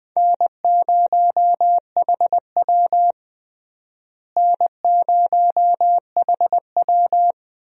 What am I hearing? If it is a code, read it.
Morse code, heard as a keyed tone, about 21 wpm, 700 Hz. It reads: N0HW N0HW
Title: USA Call Signs - 4 Characters - Repeated 2x - 20wpm